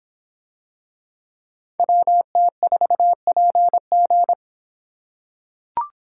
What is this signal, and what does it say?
Morse code, heard as a keyed tone, about 25 wpm, 700 Hz. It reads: WT4PG